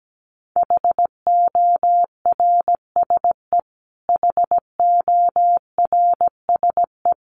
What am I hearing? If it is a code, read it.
Morse code, heard as a keyed tone, about 17 wpm, 700 Hz. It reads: HORSE HORSE
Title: Top 500 Words Repeated 17wpm